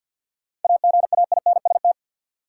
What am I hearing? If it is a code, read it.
Morse code, heard as a keyed tone, about 53 wpm, 700 Hz. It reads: AGAINST
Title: Sets of 2 Words - New Word Drill - 50wpm